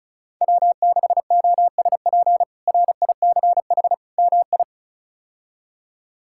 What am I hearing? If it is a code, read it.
Morse code, heard as a keyed tone, about 34 wpm, 700 Hz. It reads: W6OSP RICH MI
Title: K1USN SST QSO Practice - 35wpm